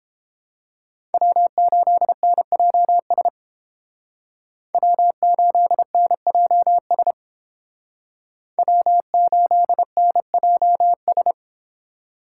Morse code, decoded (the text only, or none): W8NJS W8NJS W8NJS